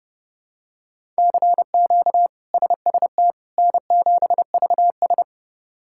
Morse code, heard as a keyed tone, about 30 wpm, 700 Hz. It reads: CQ SST N7VS
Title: K1USN SST QSO Practice - 30wpm